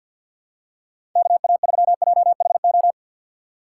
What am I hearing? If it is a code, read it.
Morse code, heard as a keyed tone, about 52 wpm, 700 Hz. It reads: KA3JHO
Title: USA Call Signs - 2x Character Spacing - 50wpm